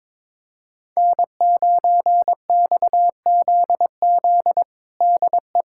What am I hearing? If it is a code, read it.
Morse code, heard as a keyed tone, about 22 wpm, 700 Hz. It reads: N9XZZ DE
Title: USA Callsign DE Callsign - 5 Characters - 22wpm